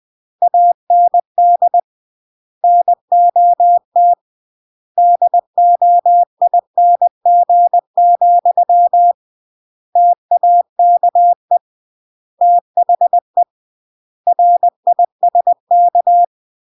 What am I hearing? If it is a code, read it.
Morse code, heard as a keyed tone, about 20 wpm, 700 Hz. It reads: AND NOT DOING, TAKE THE RISK
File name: CW / jodorowsky / 82 commandments 20-20